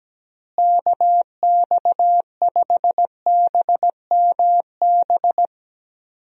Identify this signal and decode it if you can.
Morse code, heard as a keyed tone, about 17 wpm, 700 Hz. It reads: KX5BMB
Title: USA Call Signs - 6 Characters - Repeated 2x - 17wpm